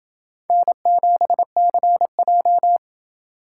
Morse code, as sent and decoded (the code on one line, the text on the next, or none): -. --... -.-. .---
N7CJ